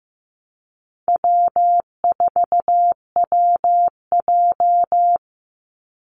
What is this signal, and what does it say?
Morse code, heard as a keyed tone, about 15 wpm, 700 Hz. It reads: W4WJ
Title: Easy Call Signs Repeated 3x - 15wpm